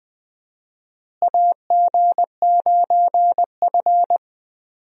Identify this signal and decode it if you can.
Morse code, heard as a keyed tone, about 20 wpm, 700 Hz. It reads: AG9F
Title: USA Call Signs - 4 Characters - Repeated 3x - 20wpm